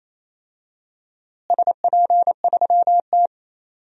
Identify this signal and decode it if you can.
Morse code, heard as a keyed tone, about 28 wpm, 700 Hz. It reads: SP3T
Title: DXCC-100 Call Signs - 4 Characters - Repeated 3x - 28wpm